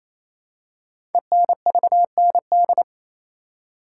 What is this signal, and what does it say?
Morse code, heard as a keyed tone, about 27 wpm, 700 Hz. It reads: ENVND
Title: Code Groups - 5 Letters - 28wpm